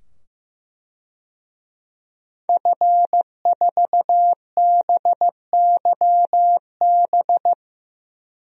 Morse code, heard as a keyed tone, about 15 wpm, 700 Hz. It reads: F4BYB
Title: DXCC-100 Call Signs - 5 Characters - 15wpm